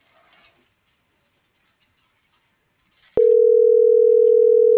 Ringback tone (North America), 440 + 480 Hz.